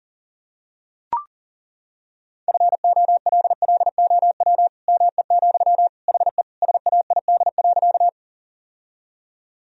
Morse code, heard as a keyed tone, about 37 wpm, 700 Hz. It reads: FOLLOW ME, HE SAID.
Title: Sentences from Top 1000 Words - Review Part 1 - 40wpm